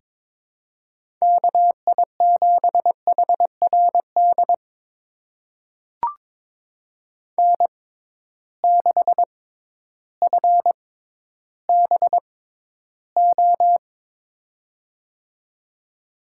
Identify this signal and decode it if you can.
Morse code, heard as a keyed tone, about 22 wpm, spaced as for 12 wpm, 700 Hz. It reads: KI7HRD N6FBO
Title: USA Call Signs - 6x Character Spacing - 22wpm